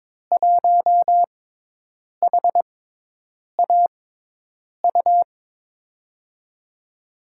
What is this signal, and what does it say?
Morse code, heard as a keyed tone, about 21 wpm, spaced as for 8 wpm, 700 Hz. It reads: 1HAU
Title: USA Call Signs - 5 Characters - 6x Character Spacing - 22wpm